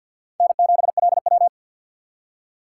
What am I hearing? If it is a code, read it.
Morse code, heard as a keyed tone, about 61 wpm, 700 Hz. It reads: N7LW
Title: Speed Racing - USA 1x2 Call Signs 50wpm